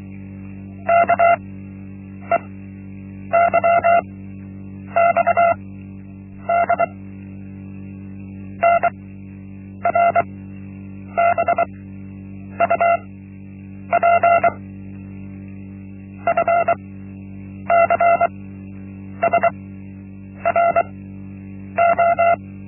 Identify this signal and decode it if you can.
Morse code, heard as a keyed tone, about 23 wpm, spaced as for 8 wpm, 680 Hz. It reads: KEYXD NRBUP FCSRO
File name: Linksmakalnis-1